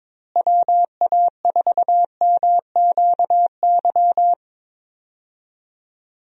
Morse code, heard as a keyed tone, about 22 wpm, 700 Hz. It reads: WA4MQY
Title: USA Call Signs - 6 Characters - Repeated 3x - 22wpm